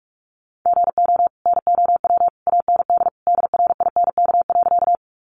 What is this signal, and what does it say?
Morse code, heard as a keyed tone, about 44 wpm, 700 Hz. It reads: GO NOW AND DRINK.
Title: Sentences from Top 1500 Words 45wpm